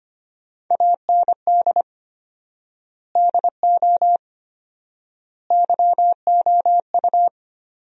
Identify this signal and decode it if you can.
Morse code, heard as a keyed tone, about 25 wpm, 700 Hz. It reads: AND DO YOU
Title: 4x Word-Spacing - Sentences from Top 100 Words - 25wpm